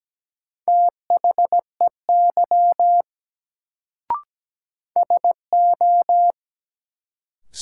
Morse code, heard as a keyed tone, about 17 wpm, 700 Hz. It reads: THEY SO
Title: Sets of 2 Words - New Word Drill - 17wpm